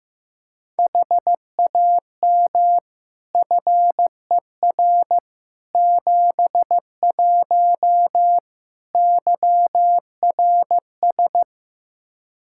Morse code, heard as a keyed tone, about 15 wpm, 700 Hz. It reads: HAM FER 71 YRS
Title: K1USN SST QSO Practice 15wpm